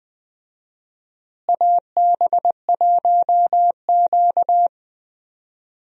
Morse code, heard as a keyed tone, about 20 wpm, 700 Hz. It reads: AB1Q
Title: USA Call Signs - 4 Characters - Repeated 3x - 20wpm